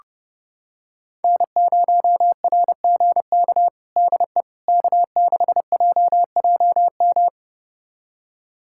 Morse code, heard as a keyed tone, about 30 wpm, 700 Hz. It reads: N0RGK DE K6JJM
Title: USA Callsign DE Callsign - 5 Characters - 30wpm